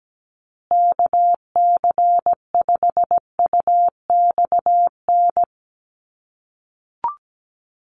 Morse code, heard as a keyed tone, about 17 wpm, 700 Hz. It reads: KC5UXN
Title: USA Call Signs Repeated 2x 17wpm